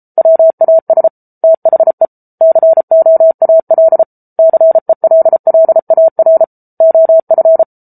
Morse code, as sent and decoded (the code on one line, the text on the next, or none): .-- .- ... / - .... / -.-. --- .- .-.. / -.-. .-.. .-.. .- .-. / --- ..-.
WAS THE COAL CELLAR OF